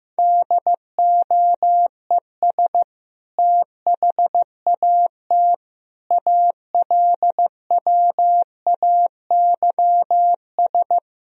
Morse code, heard as a keyed tone, about 15 wpm, 700 Hz. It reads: DOES THAT ALWAYS
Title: Sentences from Top 1000 Words - Review Part 2 - 15wpm